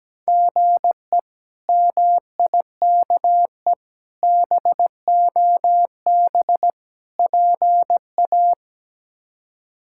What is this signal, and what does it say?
Morse code, heard as a keyed tone, about 17 wpm, 700 Hz. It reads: GE MIKE BOB PA
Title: K1USN SST QSO Practice - 17wpm